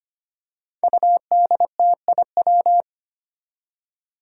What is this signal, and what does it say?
Morse code, heard as a keyed tone, about 25 wpm, 700 Hz. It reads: UDTIW
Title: Five Challenging Letter-Number Combinations 25wpm